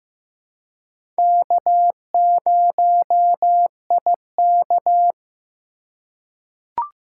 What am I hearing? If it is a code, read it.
Morse code, heard as a keyed tone, about 15 wpm, 700 Hz. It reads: K0IK
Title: USA Call Signs - 4x Character Spacing - 15wpm